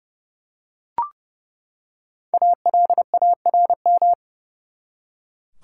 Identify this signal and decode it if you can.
Morse code, heard as a keyed tone, about 32 wpm, 700 Hz. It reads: ALARM
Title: Five Letter Words - 30wpm